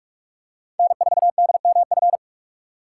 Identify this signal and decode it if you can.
Morse code, heard as a keyed tone, about 44 wpm, 690 Hz. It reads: N4DMF
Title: USA Call Signs - 5 Characters - 6x Character Spacing - 45wpm